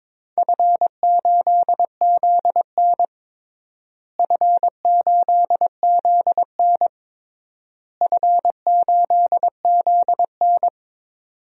Morse code, heard as a keyed tone, about 22 wpm, 700 Hz. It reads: F8ZN F8ZN F8ZN
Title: DXCC-100 Call Signs - 4 Characters - Repeated 3x - 22wpm